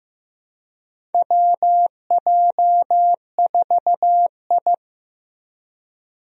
Morse code, heard as a keyed tone, about 15 wpm, 700 Hz. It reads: WJ4I
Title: USA Call Signs - 4 Characters - Repeated 3x - 15wpm